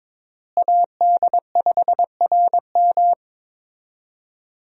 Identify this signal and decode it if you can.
Morse code, heard as a keyed tone, about 21 wpm, 700 Hz. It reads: AD5RM
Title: USA Call Signs - 5 Characters - Repeated 2x - 22wpm